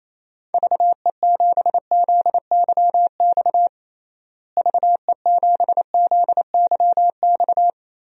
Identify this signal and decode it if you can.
Morse code, heard as a keyed tone, about 28 wpm, 700 Hz. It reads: VE7ZYX VE7ZYX
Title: DXCC-100 Call Signs - 6 Characters - Repeated 2x - 28wpm